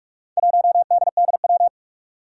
Morse code, heard as a keyed tone, about 46 wpm, 700 Hz. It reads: 1DDW
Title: Four Challenging Letter-Number Combinations 45wpm